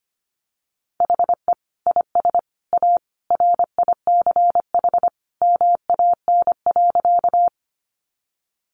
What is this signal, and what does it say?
Morse code, heard as a keyed tone, about 25 wpm, 700 Hz. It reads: HE IS A RICH MAN.
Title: Sentences from Top 1100 Words 25wpm